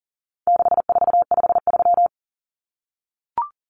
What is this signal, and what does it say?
Morse code, heard as a keyed tone, about 40 wpm, 700 Hz. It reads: 6453
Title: Numbers - 4-Digit 40wpm